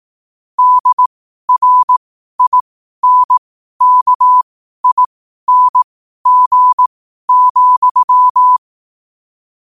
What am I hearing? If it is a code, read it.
Morse code, heard as a keyed tone, about 18 wpm, spaced as for 12 wpm, 1000 Hz. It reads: DRINKING,
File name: qotdfiles / 12wpm